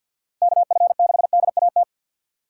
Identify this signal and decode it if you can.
Morse code, heard as a keyed tone, about 51 wpm, 700 Hz. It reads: KF6DAT
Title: USA Call Signs - 4x Character Spacing - 50wpm